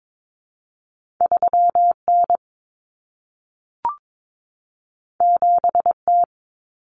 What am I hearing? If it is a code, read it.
Morse code, heard as a keyed tone, about 22 wpm, 700 Hz. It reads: 3N 7T
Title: Common Cut Numbers - 2-Digit 22wpm